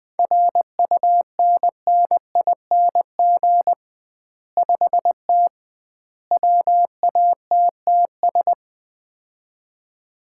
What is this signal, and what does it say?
Morse code, heard as a keyed tone, about 21 wpm, 700 Hz. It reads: RUNNING 5T WATTS
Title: Common QSO Phrases - 2x Word-Spacing - 20wpm